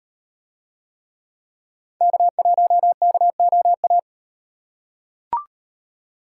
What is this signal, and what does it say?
Morse code, heard as a keyed tone, about 38 wpm, 700 Hz. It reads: K1KOA